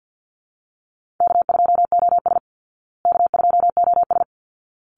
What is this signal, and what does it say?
Morse code, heard as a keyed tone, about 53 wpm, 700 Hz. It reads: K2OS K2OS
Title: Easy Call Signs Repeated 2x - 50wpm